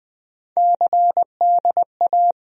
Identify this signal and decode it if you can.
Morse code, heard as a keyed tone, about 20 wpm, 700 Hz. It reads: CDA
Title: USA Call Sign Postfixes - 3 Letters 20wpm